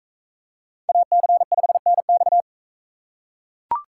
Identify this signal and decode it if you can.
Morse code, heard as a keyed tone, about 40 wpm, 700 Hz. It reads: AC5NX